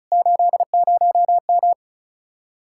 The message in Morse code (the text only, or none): 80M